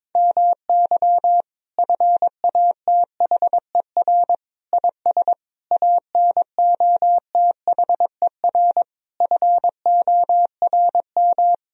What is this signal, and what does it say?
Morse code, heard as a keyed tone, about 22 wpm, 700 Hz. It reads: MY FATHER IS ANOTHER FORM
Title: Sentences from Top 1000 Words - Review Part 1 - 22wpm